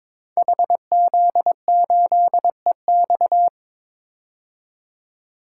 Morse code, heard as a keyed tone, about 22 wpm, 700 Hz. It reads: HZ8EX
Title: Code Groups - 5 Letter:Numbers - 22wpm